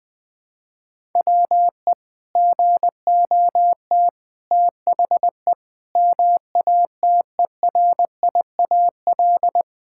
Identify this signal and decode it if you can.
Morse code, heard as a keyed tone, about 20 wpm, 700 Hz. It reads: WE GOT THE MATERIAL
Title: Sentences from Top 1000 Words - Review Part 5 - 20wpm